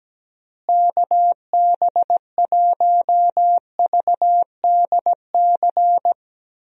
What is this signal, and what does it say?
Morse code, heard as a keyed tone, about 17 wpm, 700 Hz. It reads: KB1VDC
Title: USA Call Signs - 6 Characters - Repeated 3x - 17wpm